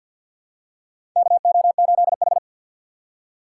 Morse code, heard as a keyed tone, about 49 wpm, 700 Hz. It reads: KO8H
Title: USA Call Signs - 6x Character Spacing - 50wpm